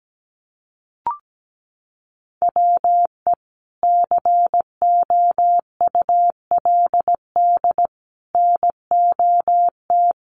Morse code, heard as a keyed tone, about 17 wpm, 700 Hz. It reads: WE COULD NOT
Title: Sentences from Top 1300 Words 17wpm